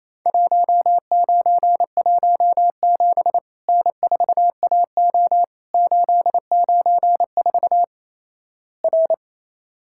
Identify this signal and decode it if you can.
Morse code, heard as a keyed tone, about 28 wpm, 700 Hz. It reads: 1917 N4AO 894 R
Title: Worked All Europe DX Contest - QTC Exchange - 28wpm